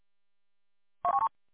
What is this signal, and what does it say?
There is no code.